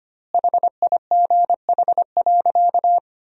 Morse code, heard as a keyed tone, about 25 wpm, 700 Hz. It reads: HIGH.
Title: 4x Word-Spacing - Sentences from Top 200 Words - 25wpm